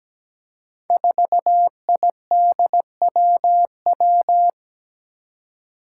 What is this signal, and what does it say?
Morse code, heard as a keyed tone, about 17 wpm, 700 Hz. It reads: 4IDWW
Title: Five Challenging Letter-Number Combinations 17wpm